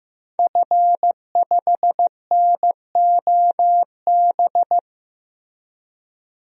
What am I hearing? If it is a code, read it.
Morse code, heard as a keyed tone, about 15 wpm, 700 Hz. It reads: F5NOB
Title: DXCC-100 Call Signs - 5 Characters - 15wpm